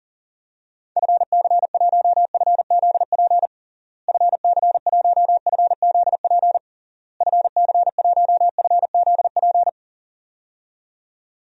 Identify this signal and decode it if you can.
Morse code, heard as a keyed tone, about 40 wpm, 700 Hz. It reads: FC1FZP FC1FZP FC1FZP
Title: DXCC-100 Call Signs - 6 Characters - Repeated 3x - 40wpm